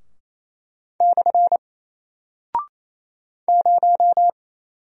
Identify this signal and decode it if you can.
Morse code, heard as a keyed tone, about 28 wpm, 700 Hz. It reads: / 0
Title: Single Character (Letter:Number:Punctuation:Prosign) Alt 28wpm